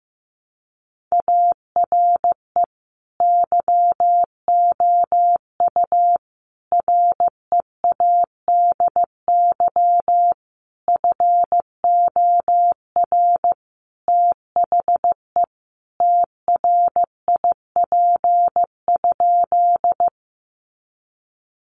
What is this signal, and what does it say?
Morse code, heard as a keyed tone, about 15 wpm, 700 Hz. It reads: ARE YOU READY FOR THE TRIP?
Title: Sentences from Top 1300 Words 15wpm